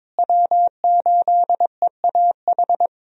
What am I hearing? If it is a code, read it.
Morse code, heard as a keyed tone, about 22 wpm, 700 Hz. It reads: W8EAH
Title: USA Call Signs - 5 Characters - Repeated 2x - 22wpm